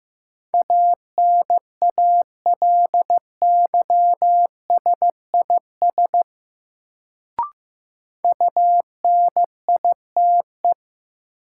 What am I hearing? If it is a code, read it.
Morse code, heard as a keyed tone, about 15 wpm, 700 Hz. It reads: ANALYSIS UNITE
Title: Sentences from Top 1000 Words - Review Part 5 - New Word Drill - 15wpm